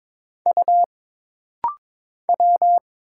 Morse code, heard as a keyed tone, about 22 wpm, 700 Hz. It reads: U W
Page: This rapid-fire timing is geared to push your ICR (Instant Character Recognition) skills.